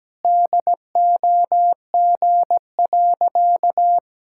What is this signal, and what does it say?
Morse code, heard as a keyed tone, about 17 wpm, 700 Hz. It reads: DOG.
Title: Pangram Challenge v2 - 17wpm